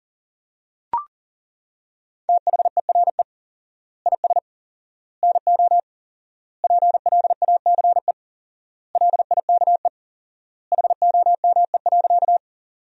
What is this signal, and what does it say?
Morse code, heard as a keyed tone, about 40 wpm, 700 Hz. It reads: THERE IS NO PLACE LIKE HOME.
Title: 4x Word-Spacing - Sentences from Top 200 Words - 40wpm